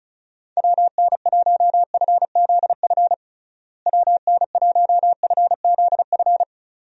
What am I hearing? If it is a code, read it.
Morse code, heard as a keyed tone, about 34 wpm, 700 Hz. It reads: WN1FZF WN1FZF
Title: USA Call Signs - 6 Characters - Repeated 2x - 35wpm